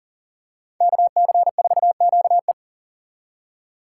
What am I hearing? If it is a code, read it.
Morse code, heard as a keyed tone, about 38 wpm, 700 Hz. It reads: KC4QE
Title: USA Call Signs - 5 Characters - 4x Character Spacing - 40wpm